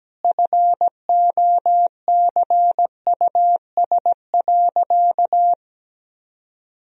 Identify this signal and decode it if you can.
Morse code, heard as a keyed tone, about 17 wpm, 700 Hz. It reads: FOCUS.
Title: Sentences from Top 1000 Words - Review Part 4 - 17wpm